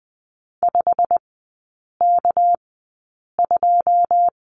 Morse code, heard as a keyed tone, about 20 wpm, spaced as for 8 wpm, 690 Hz. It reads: 5K2
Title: 198 Lesson 32e - Letter k - Three Characters 20wpm